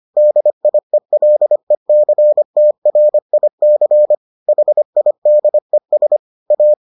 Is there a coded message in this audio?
Yes: Morse code from 0.2 to 6.7 s, heard as a keyed tone, about 25 wpm, 600 Hz. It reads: DIELECTRIC HIDES A